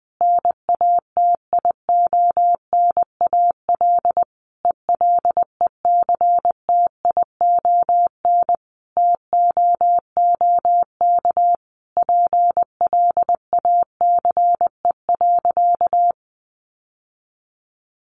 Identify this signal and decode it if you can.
Morse code, heard as a keyed tone, about 20 wpm, 700 Hz. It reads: NATIONAL ELECTION TOOK PLACE.
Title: Sentences from Top 1500 Words 20wpm